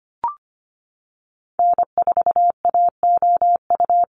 Morse code, heard as a keyed tone, about 25 wpm, 700 Hz. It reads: N4AOU
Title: USA Call Signs Repeated 2x 25wpm